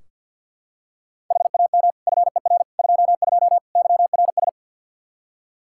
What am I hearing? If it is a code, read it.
Morse code, heard as a keyed tone, about 47 wpm, 700 Hz. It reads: HAM FER 22 YRS
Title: K1USN SST QSO Practice 50wpm